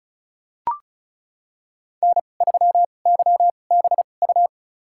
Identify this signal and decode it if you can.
Morse code, heard as a keyed tone, about 34 wpm, spaced as for 25 wpm, 700 Hz. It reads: N3YBU
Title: USA Call Signs - 5 Characters - 2x Character Spacing - 35wpm